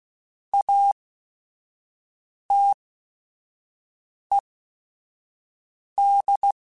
Morse code, heard as a keyed tone, about 16 wpm, spaced as for 5 wpm, 800 Hz. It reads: ATED
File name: Technical Subjects / Technical Subjects-mp3s / XC @5